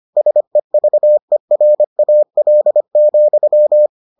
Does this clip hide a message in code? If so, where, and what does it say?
Yes: Morse code from 0.2 to 3.9 s, heard as a keyed tone, about 25 wpm, 600 Hz. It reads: SEVERAL,